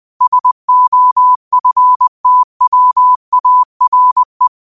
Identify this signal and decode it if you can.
Morse code, heard as a keyed tone, about 20 wpm, 1000 Hz. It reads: SOFTWARE
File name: qotdfiles / 20wpm